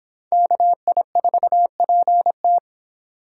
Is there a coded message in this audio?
Yes: Morse code from 0.3 to 2.6 s, heard as a keyed tone, about 25 wpm, 700 Hz. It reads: KI4PT